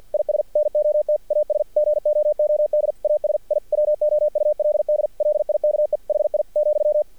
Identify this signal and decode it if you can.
Morse code, heard as a keyed tone, about 50 wpm, 600 Hz. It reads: IS NOT AS GOOD AS I WOULD LIKE HI,
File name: archived / CRSnet